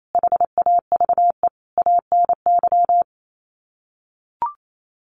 Morse code, heard as a keyed tone, about 28 wpm, 700 Hz. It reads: HAVE ANY
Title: Sets of 5 Words Encore 28wpm